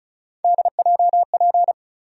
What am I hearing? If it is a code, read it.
Morse code, heard as a keyed tone, about 36 wpm, 700 Hz. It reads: DJP
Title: USA Call Sign Postfixes - 3 Letters 35wpm